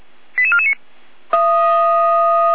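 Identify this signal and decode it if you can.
Five-tone selective call (ZVEI), heard as quick runs of five tones, tones of about 80 ms. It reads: 88449
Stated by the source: ZVEI five-tone call 88252 received by SDR.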